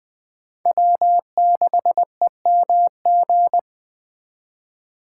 Morse code, heard as a keyed tone, about 25 wpm, 700 Hz. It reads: W6EMG
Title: Speed Racing - USA 1x3 Call Signs 20wpm